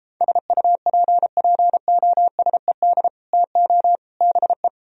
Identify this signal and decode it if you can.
Morse code, heard as a keyed tone, about 33 wpm, 700 Hz. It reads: SUPPOSED TO BE